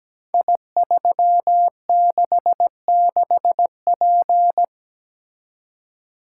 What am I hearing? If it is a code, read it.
Morse code, heard as a keyed tone, about 17 wpm, 700 Hz. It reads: I366P
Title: Code Groups - 5 Letter:Numbers - 17wpm